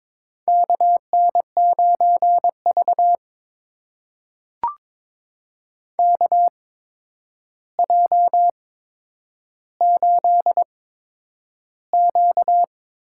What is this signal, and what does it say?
Morse code, heard as a keyed tone, about 22 wpm, spaced as for 9 wpm, 700 Hz. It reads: KN9V KJ8Q
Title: USA Call Signs - 4 Characters - 8x Character Spacing - 22wpm